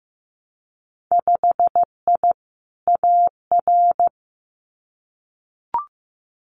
Morse code, heard as a keyed tone, about 15 wpm, 700 Hz. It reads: 5I AR
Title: Winter Field Day Exchanges Encore 15wpm